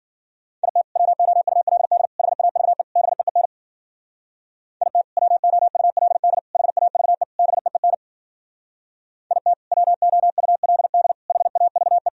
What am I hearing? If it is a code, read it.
Morse code, heard as a keyed tone, about 53 wpm, 700 Hz. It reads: IT WOULD HAVE BEEN IT WOULD HAVE BEEN IT WOULD HAVE